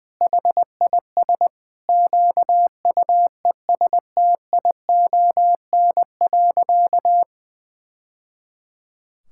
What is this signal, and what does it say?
Morse code, heard as a keyed tone, about 20 wpm, 700 Hz. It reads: HIS QUESTION.
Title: Sentences from Top 1000 Words - Review Part 6 - 20wpm